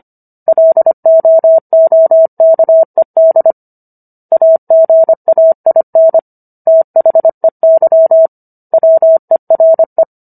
Morse code, heard as a keyed tone, about 25 wpm, 650 Hz. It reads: LOOKED AGAIN THEY WERE